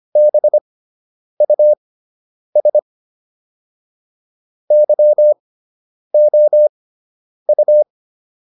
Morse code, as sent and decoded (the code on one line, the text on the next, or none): -... ..- ... / -.-- --- ..-
BUS YOU